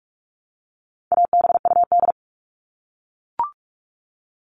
Morse code, heard as a keyed tone, about 44 wpm, 700 Hz. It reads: ABUD
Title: All Cut Numbers - 4-Digit 45wpm